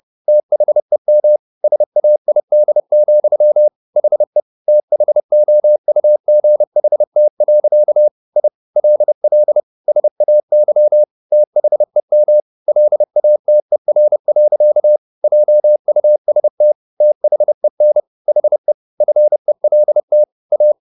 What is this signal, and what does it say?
Morse code, heard as a keyed tone, about 30 wpm, 600 Hz. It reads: THEM SAID, HE THOUGHT. I LL SAY THEM LATER. JUST THEN HE FELT A